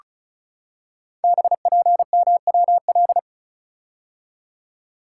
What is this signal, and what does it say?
Morse code, heard as a keyed tone, about 33 wpm, 690 Hz. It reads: BPMWL
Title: Code Groups - 5 Letters - 35wpm